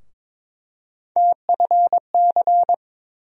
Morse code, heard as a keyed tone, about 22 wpm, 700 Hz. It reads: TFC
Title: USA Call Sign Postfixes - 3 Letters 22wpm